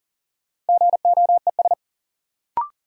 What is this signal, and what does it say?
Morse code, heard as a keyed tone, about 40 wpm, 700 Hz. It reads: GOES
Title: Sentences from Top 1500 Words - New Word Drill Alt - 40wpm